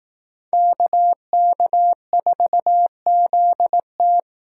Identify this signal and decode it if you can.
Morse code, heard as a keyed tone, about 18 wpm, 700 Hz. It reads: KK4ZT
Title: Speed Racing - USA Call Signs 15wpm